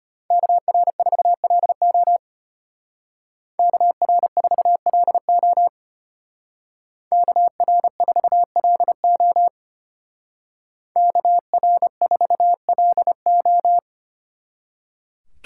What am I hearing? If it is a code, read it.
Morse code, heard as a keyed tone, about 29 wpm, 700 Hz. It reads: KR4LO KR4LO KR4LO KR4LO
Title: Speed Racing - USA 2x2 Call Signs 25wpm